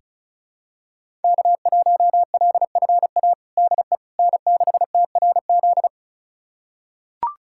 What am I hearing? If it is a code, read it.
Morse code, heard as a keyed tone, about 36 wpm, 700 Hz. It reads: K1LFA DE N6TRZ
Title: USA Callsign DE Callsign - 5 Characters - 35wpm